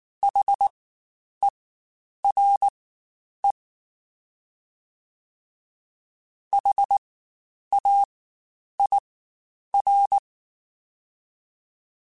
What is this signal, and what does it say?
Morse code, heard as a keyed tone, about 19 wpm, spaced as for 9 wpm, 800 Hz. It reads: HERE HAIR
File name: Spaced Alphas / Spaced Alphas mp3s / Hs sp @18